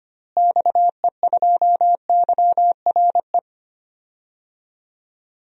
Morse code, heard as a keyed tone, about 25 wpm, 700 Hz. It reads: XE2YRE
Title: DXCC-100 Call Signs - 6 Characters - Repeated 2x - 25wpm